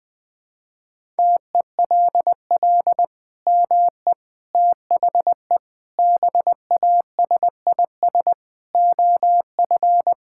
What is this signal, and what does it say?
Morse code, heard as a keyed tone, about 21 wpm, 700 Hz. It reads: TELL ME THE BASIS OF
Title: Sentences from Top 1000 Words - Review Part 5 - 20wpm